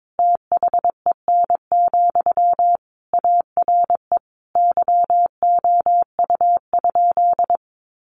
Morse code, heard as a keyed tone, about 22 wpm, 700 Hz. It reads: THEN, ARE YOU?
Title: Sentences from Top 100 Words 22wpm